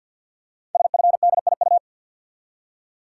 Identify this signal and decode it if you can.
Morse code, heard as a keyed tone, about 51 wpm, 700 Hz. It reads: SFDIU
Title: Code Groups - 5 Letters - 50wpm